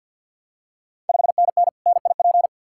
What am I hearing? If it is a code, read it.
Morse code, heard as a keyed tone, about 47 wpm, 700 Hz. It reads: 5NN NIP
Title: Ontario QSO Party Exchange 50wpm